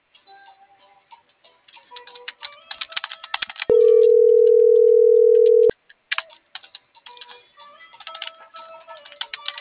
Ringback tone (North America), 440 + 480 Hz.